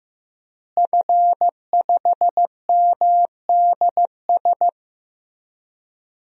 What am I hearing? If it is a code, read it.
Morse code, heard as a keyed tone, about 15 wpm, 700 Hz. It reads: F5MDS
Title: DXCC-100 Call Signs - 5 Characters - Repeated 3x - 15wpm